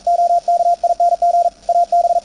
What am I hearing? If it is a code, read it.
Morse code, heard as a keyed tone, about 41 wpm, 690 Hz, with radio noise. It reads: OKING AB
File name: attachments / CW 20m 200Hz